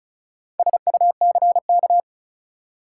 Morse code, heard as a keyed tone, about 36 wpm, 700 Hz. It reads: SUCK
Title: Four Letter Words - 35wpm